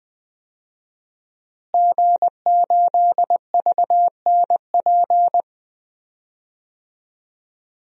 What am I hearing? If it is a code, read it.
Morse code, heard as a keyed tone, about 20 wpm, 700 Hz. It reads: G8VNP
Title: DXCC-100 Call Signs - 5 Characters - Repeated 3x - 20wpm